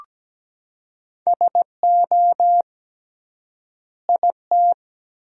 Morse code, heard as a keyed tone, about 17 wpm, 700 Hz. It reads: SO IT
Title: 3x Word-Spacing - Sentences from Top 300 Words - 17wpm